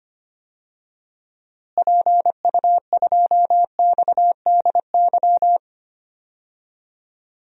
Morse code, heard as a keyed tone, about 25 wpm, 700 Hz. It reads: PU2XDY
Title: DXCC-100 Call Signs - 6 Characters - Repeated 2x - 25wpm